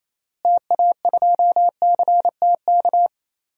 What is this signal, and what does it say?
Morse code, heard as a keyed tone, about 28 wpm, 700 Hz. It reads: TA2CTK
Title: DXCC-100 Call Signs - 6 Characters - Repeated 3x - 28wpm